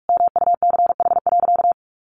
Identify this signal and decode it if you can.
Morse code, heard as a keyed tone, about 44 wpm, 700 Hz. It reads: MUCH.
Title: Sentences from Top 1500 Words 45wpm